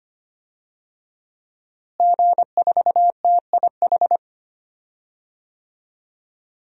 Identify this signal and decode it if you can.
Morse code, heard as a keyed tone, about 25 wpm, 700 Hz. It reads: G4TIH
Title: DXCC-100 Call Signs - 5 Characters - Repeated 2x - 25wpm